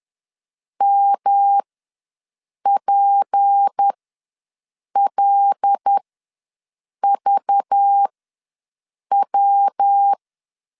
Morse code, heard as a keyed tone, about 10 wpm, spaced as for 6 wpm, 800 Hz. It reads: MPLVW
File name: LEZIONI / 6 PKX0 / 6 PK ws 10wpm 3745